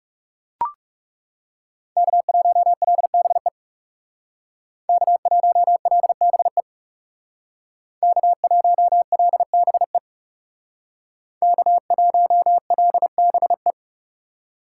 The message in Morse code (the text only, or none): K1LBE K1LBE K1LBE K1LBE